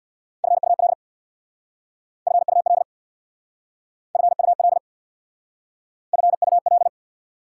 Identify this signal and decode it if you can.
Morse code, heard as a keyed tone, about 61 wpm, 690 Hz. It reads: FUL FUL FUL FUL
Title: Speed-Racing Prefixes and Suffixes 50wpm